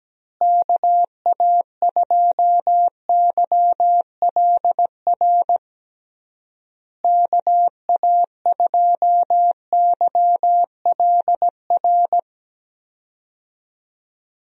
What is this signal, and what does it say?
Morse code, heard as a keyed tone, about 17 wpm, 700 Hz. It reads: KA2YLR KA2YLR
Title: USA Call Signs - 6 Characters - Repeated 3x - 17wpm